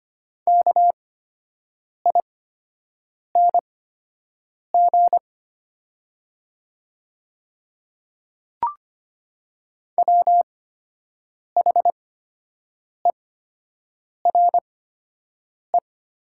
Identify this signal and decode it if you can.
Morse code, heard as a keyed tone, about 25 wpm, spaced as for 7 wpm, 700 Hz. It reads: KING WHERE
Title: 8x Character Spacing - Top 500 Words - 25wpm